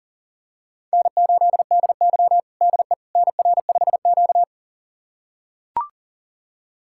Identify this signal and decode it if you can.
Morse code, heard as a keyed tone, about 40 wpm, 700 Hz. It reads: N8DY DE NR5Q
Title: USA Callsign DE Callsign - 4 Characters - 40wpm